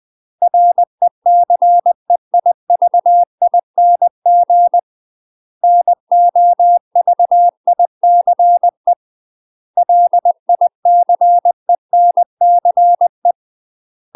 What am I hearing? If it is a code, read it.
Morse code, heard as a keyed tone, about 20 wpm, 700 Hz. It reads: RECEIVING NOVICE LICENCE
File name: UK MASTER QSOs TRAINING / 20WPM / UK MASTER QSOs 0036